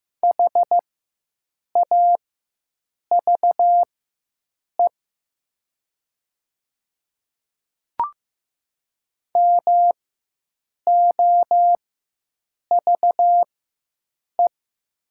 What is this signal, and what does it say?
Morse code, heard as a keyed tone, about 15 wpm, spaced as for 7 wpm, 700 Hz. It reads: HAVE MOVE
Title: 4x Character Spacing - Top 200 Words - 15wpm